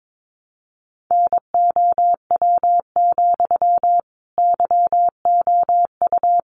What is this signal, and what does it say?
Morse code, heard as a keyed tone, about 22 wpm, 700 Hz. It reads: NOW, YOU
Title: Sentences from Top 200 Words 22wpm